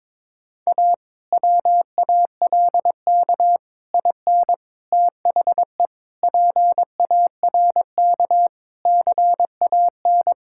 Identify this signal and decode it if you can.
Morse code, heard as a keyed tone, about 22 wpm, 700 Hz. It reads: A WALK IN THE PARK CAN
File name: Difficult Sentences with Hints - 22wpm